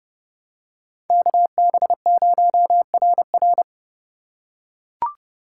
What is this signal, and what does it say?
Morse code, heard as a keyed tone, about 30 wpm, 700 Hz. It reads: KB0RR
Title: USA Call Signs - 5 Characters - 4x Character Spacing - 30wpm